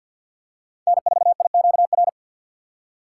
Morse code, heard as a keyed tone, about 53 wpm, 700 Hz. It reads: N4IQR
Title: USA Call Signs - 4x Character Spacing - 50wpm